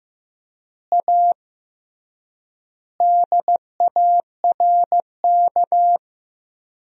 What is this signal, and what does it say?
Morse code, heard as a keyed tone, about 15 wpm, 700 Hz. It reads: A DARK
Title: 3x Word-Spacing - Sentences from Top 500 Words - 15wpm